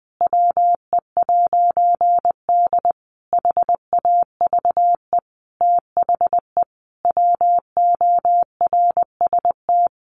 Morse code, heard as a keyed tone, about 20 wpm, 700 Hz. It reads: WE'D HAVE THE WORST
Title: A Sound Of Thunder 20wpm